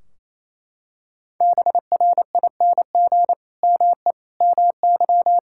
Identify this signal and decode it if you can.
Morse code, heard as a keyed tone, about 28 wpm, 700 Hz. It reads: BRING ME MY
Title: Sentences from Top 1600 Words 28wpm